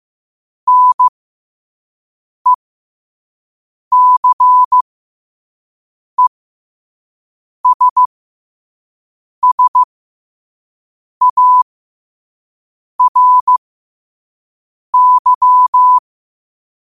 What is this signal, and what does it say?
Morse code, heard as a keyed tone, about 15 wpm, spaced as for 5 wpm, 1000 Hz. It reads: NECESSARY
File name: qotdfiles / 5wpm